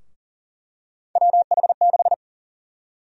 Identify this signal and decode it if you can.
Morse code, heard as a keyed tone, about 40 wpm, 700 Hz. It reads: WH6
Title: USA Call Sign Prefixes - 40wpm